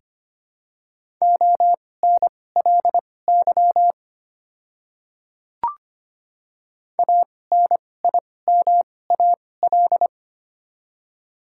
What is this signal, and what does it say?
Morse code, heard as a keyed tone, about 25 wpm, spaced as for 18 wpm, 690 Hz. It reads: ONLY ANIMAL
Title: 2x Character Spacing - Top 200 Words - 25wpm